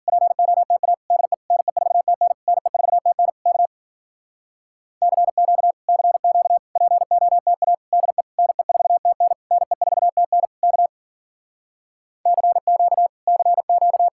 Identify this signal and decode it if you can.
Morse code, heard as a keyed tone, about 49 wpm, 700 Hz. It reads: POTA DE NE4TN NE4TN K CQ CQ POTA DE NE4TN NE4TN K CQ CQ
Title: Speed-Racing Common QSO Phrases Encore - 40wpm